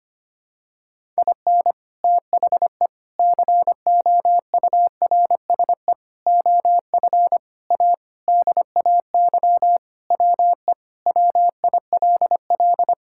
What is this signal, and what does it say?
Morse code, heard as a keyed tone, about 25 wpm, 700 Hz. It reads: IN THE COURSE OF A DAY WE WILL
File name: practice / 25wpm / Sentences with 3-Word Phrases - 25wpm